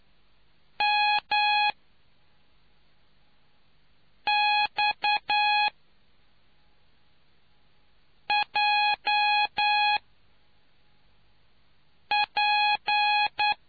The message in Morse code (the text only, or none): M X J P